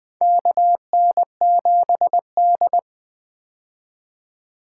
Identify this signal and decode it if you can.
Morse code, heard as a keyed tone, about 21 wpm, 700 Hz. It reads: KN7D
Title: 2023 POTA Hunter Call Signs - 20wpm